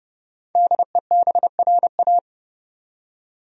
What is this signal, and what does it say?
Morse code, heard as a keyed tone, about 30 wpm, 700 Hz. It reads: DEBRA
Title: USA Common Names - 30wpm